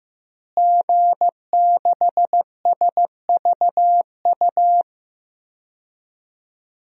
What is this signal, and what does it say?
Morse code, heard as a keyed tone, about 15 wpm, 700 Hz. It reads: G6SVU
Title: DXCC-100 Call Signs - 5 Characters - Repeated 3x - 15wpm